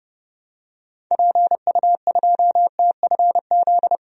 Morse code, heard as a keyed tone, about 29 wpm, 700 Hz. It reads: PU2TFZ
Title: DXCC-100 Call Signs - 6 Characters - Repeated 2x - 30wpm